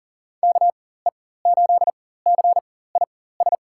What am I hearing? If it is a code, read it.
Morse code, heard as a keyed tone, about 40 wpm, spaced as for 19 wpm, 700 Hz. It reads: KE8CIS
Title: USA Call Signs - 6 Characters - 4x Character Spacing - 40wpm